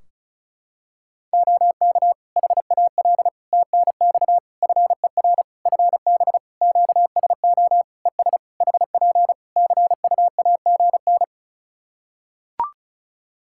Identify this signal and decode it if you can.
Morse code, heard as a keyed tone, about 34 wpm, 700 Hz. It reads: OK HAL TNX FER FB QSO ES HP CUAGN
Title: Practice QSO - Exchange 4 - Ending - 35wpm